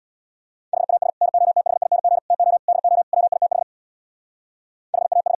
Morse code, heard as a keyed tone, about 61 wpm, 700 Hz. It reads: HAS NOTHING TO DO WITH HAS
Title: Speed-Racing - Sets of 5 Words 50wpm